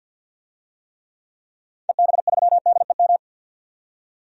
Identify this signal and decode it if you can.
Morse code, heard as a keyed tone, about 47 wpm, 700 Hz. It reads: EB3DEM
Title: DXCC-100 Call Signs - 6 Characters - 50wpm